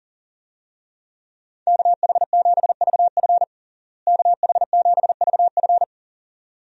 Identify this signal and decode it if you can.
Morse code, heard as a keyed tone, about 37 wpm, 700 Hz. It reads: KH7VF KH7VF
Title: USA Call Signs - 5 Characters - Repeated 2x - 40wpm